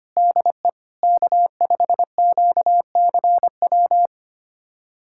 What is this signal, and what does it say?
Morse code, heard as a keyed tone, about 25 wpm, 700 Hz. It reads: DE K5QCW
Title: USA Callsign DE Callsign - 5 Characters - 25wpm